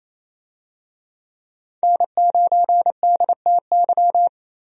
Morse code, heard as a keyed tone, about 28 wpm, 700 Hz. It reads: N9DTY
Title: USA Call Signs - 5 Characters - Repeated 3x - 28wpm